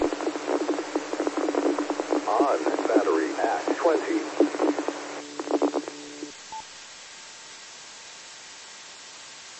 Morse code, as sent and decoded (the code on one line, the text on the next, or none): none